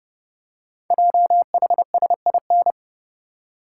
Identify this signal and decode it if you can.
Morse code, heard as a keyed tone, about 30 wpm, 690 Hz. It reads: JHSIN
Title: Code Groups - 5 Letters - 30wpm